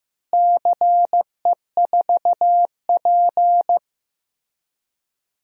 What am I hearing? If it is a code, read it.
Morse code, heard as a keyed tone, about 15 wpm, 700 Hz. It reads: CE4P
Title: DXCC-100 Call Signs - 4 Characters - Repeated 3x - 15wpm